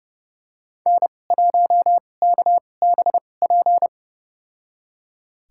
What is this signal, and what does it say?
Morse code, heard as a keyed tone, about 31 wpm, spaced as for 22 wpm, 700 Hz. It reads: N1KBP
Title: USA Call Signs - 5 Characters - 2x Character Spacing - 30wpm